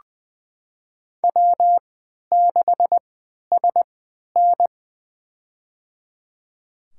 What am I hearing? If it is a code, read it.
Morse code, heard as a keyed tone, about 20 wpm, spaced as for 11 wpm, 700 Hz. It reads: W6SN
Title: USA Call Signs - 3x Character Spacing - 20wpm